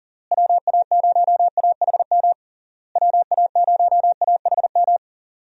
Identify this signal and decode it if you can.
Morse code, heard as a keyed tone, about 40 wpm, 700 Hz. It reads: WA0AHM WA0AHM
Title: USA Call Signs - 6 Characters - Repeated 3x - 40wpm